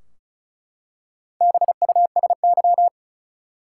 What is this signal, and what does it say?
Morse code, heard as a keyed tone, about 36 wpm, 700 Hz. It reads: BUSY
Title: Four Letter Words - 35wpm